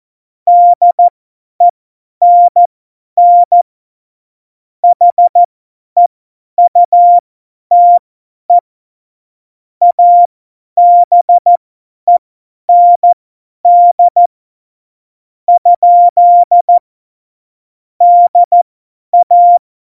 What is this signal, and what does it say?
Morse code, heard as a keyed tone, about 14 wpm, spaced as for 10 wpm, 700 Hz. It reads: DENN HEUTE ABEND ? DA
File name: Morsekurs / Witze / Witz 31.Juli 2018, 50 F70